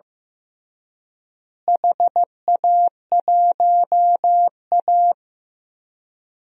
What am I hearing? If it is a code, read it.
Morse code, heard as a keyed tone, about 15 wpm, 700 Hz. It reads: HA1A
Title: DXCC-100 Call Signs - 4 Characters - Repeated 2x - 15wpm